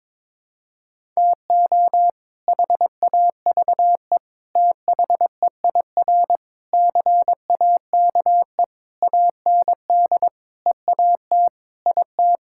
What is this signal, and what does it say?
Morse code, heard as a keyed tone, about 22 wpm, 700 Hz. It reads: TO HAVE THEIR CAKE AND EAT IT
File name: practice / 22wpm / Sets of 8 Words - 22wpm